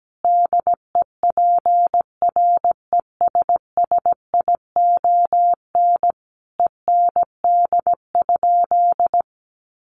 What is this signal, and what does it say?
Morse code, heard as a keyed tone, about 17 wpm, 700 Hz. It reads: DEPRESSION END?
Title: Sentences from Top 1300 Words 17wpm